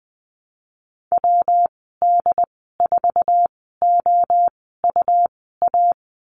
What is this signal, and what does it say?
Morse code, heard as a keyed tone, about 20 wpm, spaced as for 14 wpm, 700 Hz. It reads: WD4OUA
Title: 147 Lesson 24g - Letter y - Call Signs 20wpm